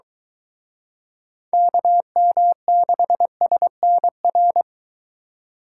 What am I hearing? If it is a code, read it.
Morse code, heard as a keyed tone, about 23 wpm, 700 Hz. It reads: KM6SNR